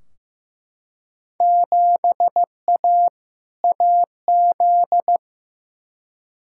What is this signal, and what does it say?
Morse code, heard as a keyed tone, about 15 wpm, 700 Hz. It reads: 7A AZ
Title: Field Day Exchanges - 15wpm